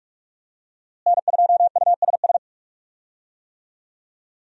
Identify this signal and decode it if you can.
Morse code, heard as a keyed tone, about 46 wpm, 700 Hz. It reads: N2USS
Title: USA Call Signs - 5 Characters - Repeated 2x - 45wpm